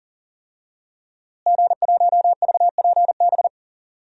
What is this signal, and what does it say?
Morse code, heard as a keyed tone, about 38 wpm, 700 Hz. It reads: G1VPB
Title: DXCC-100 Call Signs - 5 Characters - Repeated 3x - 40wpm